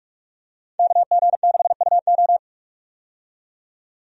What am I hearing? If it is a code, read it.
Morse code, heard as a keyed tone, about 42 wpm, 700 Hz. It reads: KG6UO
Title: USA Call Signs - 5 Characters - 45wpm